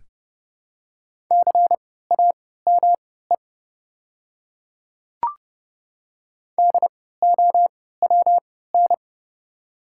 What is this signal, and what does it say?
Morse code, heard as a keyed tone, about 30 wpm, spaced as for 17 wpm, 700 Hz. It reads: CAME DOWN
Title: 3x Character Spacing - Top 200 Words - 30wpm